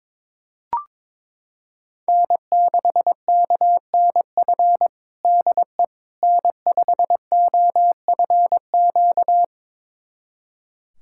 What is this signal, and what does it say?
Morse code, heard as a keyed tone, about 22 wpm, 700 Hz. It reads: N6KNF DE N5OFQ
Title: USA Callsign DE Callsign - 5 Characters - 22wpm